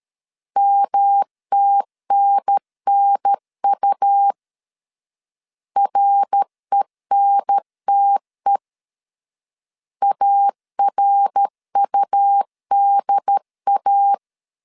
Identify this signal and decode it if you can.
Morse code, heard as a keyed tone, about 13 wpm, 800 Hz. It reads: MTNNU RENTE ARUDA
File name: CW / DET / 1 3 1500C S C 12wpm 2750